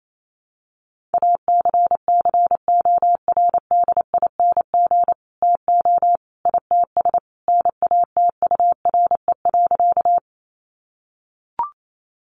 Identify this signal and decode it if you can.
Morse code, heard as a keyed tone, about 28 wpm, 700 Hz. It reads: ACCORDING TO ITS NATURE.
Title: Sentences from Top 700 Words 28wpm